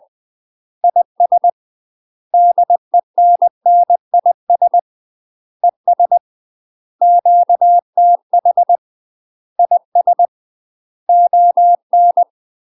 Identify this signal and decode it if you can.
Morse code, heard as a keyed tone, about 20 wpm, 700 Hz. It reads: IS DENNIS ES QTH IS ON
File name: UK MASTER QSOs TRAINING / 20WPM / UK MASTER QSOs 0013